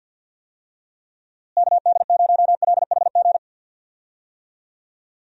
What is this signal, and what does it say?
Morse code, heard as a keyed tone, about 49 wpm, 700 Hz. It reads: KD0LHG
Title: USA Call Signs - 6 Characters - Repeated 2x - 50wpm